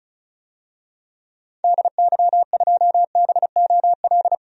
Morse code, heard as a keyed tone, about 36 wpm, 700 Hz. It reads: DY2BOL
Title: DXCC-100 Call Signs - 6 Characters - 35wpm